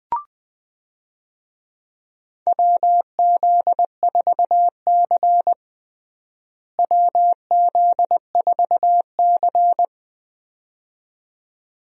Morse code, heard as a keyed tone, about 21 wpm, 700 Hz. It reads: WZ4C WZ4C
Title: USA Call Signs - 4 Characters - Repeated 2x - 20wpm